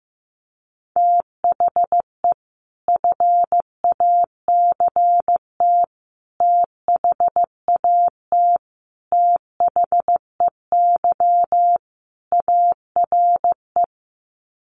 Morse code, heard as a keyed tone, about 15 wpm, 700 Hz. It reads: THE FACT THAT THEY ARE
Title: Sets of 5 Words Encore2 15wpm